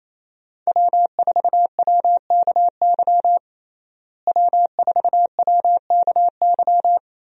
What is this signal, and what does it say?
Morse code, heard as a keyed tone, about 28 wpm, 700 Hz. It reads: W4WKY W4WKY
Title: USA Call Signs - 5 Characters - Repeated 2x - 28wpm